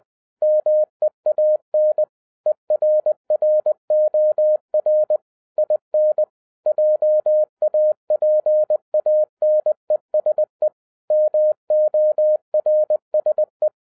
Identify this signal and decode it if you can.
Morse code, heard as a keyed tone, about 20 wpm, 610 Hz. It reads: MEAN ERROR IN JAPANESE MORSE